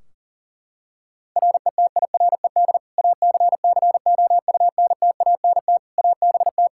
Morse code, heard as a keyed tone, about 37 wpm, 700 Hz. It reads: RETIRED ACCOUNTANT ABT
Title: K1USN SST QSO Practice 40wpm